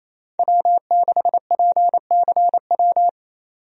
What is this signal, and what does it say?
Morse code, heard as a keyed tone, about 28 wpm, 700 Hz. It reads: W6PCW
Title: USA Call Signs - 5 Characters - 28wpm